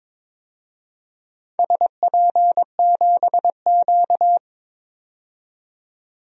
Morse code, heard as a keyed tone, about 22 wpm, 700 Hz. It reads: SP7Q
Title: DXCC-100 Call Signs - 4 Characters - 22wpm